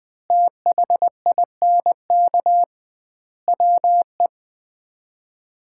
Morse code, heard as a keyed tone, about 21 wpm, 700 Hz. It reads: THINK WE
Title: 2x Word-Spacing - Sets of 4 Words - 20wpm